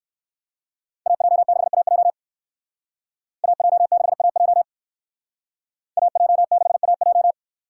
Morse code, heard as a keyed tone, about 57 wpm, 700 Hz. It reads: AJ6AJ AJ6AJ AJ6AJ